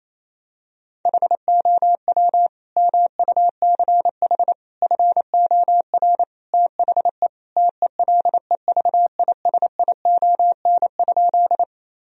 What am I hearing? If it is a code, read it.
Morse code, heard as a keyed tone, about 28 wpm, 700 Hz. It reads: HOW MUCH FOR THE TELEVISION?
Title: Sentences from Top 1000 Words - Review Part 4 - 28wpm